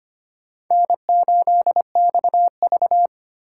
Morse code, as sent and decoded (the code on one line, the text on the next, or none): -. ---.. -..- ...-
N8XV